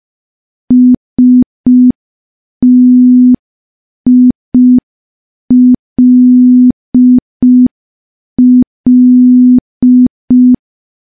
Morse code, heard as a keyed tone, about 5 wpm, 250 Hz. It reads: STILL
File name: morse / morse-training-05WPM